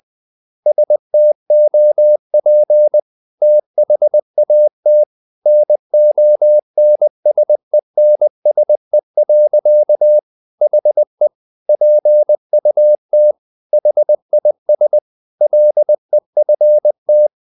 Morse code, heard as a keyed tone, about 20 wpm, 600 Hz. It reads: STOP THAT NONSENSE. HE PUT HIS LEFT
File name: the old man and the sea / the old man and the sea part 1 20wpm